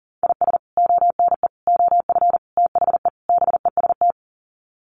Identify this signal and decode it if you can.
Morse code, heard as a keyed tone, about 40 wpm, 700 Hz. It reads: IS ONE OF THE BEST
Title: Sets of 5 Words 40wpm